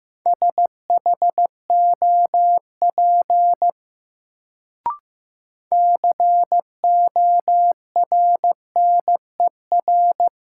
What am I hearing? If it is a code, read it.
Morse code, heard as a keyed tone, about 15 wpm, 700 Hz. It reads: SHOP CORNER
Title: Sentences from Top 1600 Words - New Word Drill Alt - 15wpm